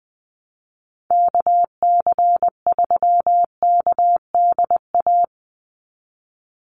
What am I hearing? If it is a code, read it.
Morse code, heard as a keyed tone, about 20 wpm, 700 Hz. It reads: KC3KDA
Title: USA Call Signs 20wpm